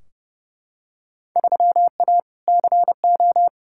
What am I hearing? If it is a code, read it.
Morse code, heard as a keyed tone, about 30 wpm, 700 Hz. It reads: 3A CO
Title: Field Day Exchanges Encore - 30wpm